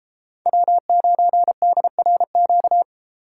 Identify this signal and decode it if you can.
Morse code, heard as a keyed tone, about 33 wpm, 700 Hz. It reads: W9DRQ
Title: Speed Racing - USA 1x3 Call Signs 28wpm